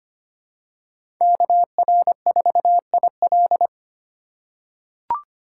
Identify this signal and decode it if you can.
Morse code, heard as a keyed tone, about 25 wpm, 700 Hz. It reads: KR4IL
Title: USA Call Signs - 5 Characters - 4x Character Spacing - 25wpm